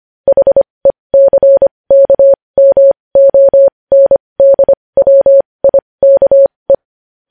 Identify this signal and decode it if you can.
Morse code, heard as a keyed tone, about 25 wpm, spaced as for 20 wpm, 550 Hz. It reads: HECKMONDWIKE
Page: news-headlines-in-morse-code-at-20-wpm